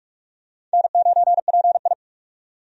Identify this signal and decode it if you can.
Morse code, heard as a keyed tone, about 41 wpm, 700 Hz. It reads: N9PI
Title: USA Call Signs - 4 Characters - 45wpm